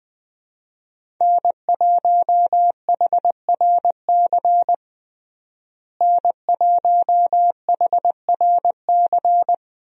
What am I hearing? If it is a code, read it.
Morse code, heard as a keyed tone, about 20 wpm, 700 Hz. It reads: N1HRC N1HRC
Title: USA Call Signs - 5 Characters - Repeated 3x - 20wpm